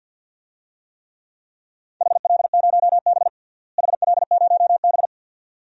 Morse code, heard as a keyed tone, about 51 wpm, 700 Hz. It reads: HL0B HL0B
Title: DXCC-100 Call Signs - 4 Characters - Repeated 2x - 50wpm